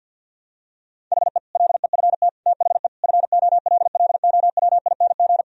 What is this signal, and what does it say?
Morse code, heard as a keyed tone, about 46 wpm, 700 Hz. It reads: HE LEFT THE FOLLOWING
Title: Sentences from Top 1000 Words - Review Part 4 - 50wpm